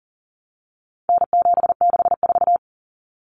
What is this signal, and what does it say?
Morse code, heard as a keyed tone, about 40 wpm, 700 Hz. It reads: N764
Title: Common Cut Numbers - 4-Digit 40wpm